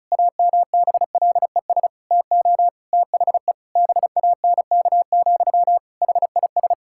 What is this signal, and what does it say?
Morse code, heard as a keyed tone, about 36 wpm, 700 Hz. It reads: AMBLES TO THE BANK, HIS
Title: Pangram Challenge v2 - 35wpm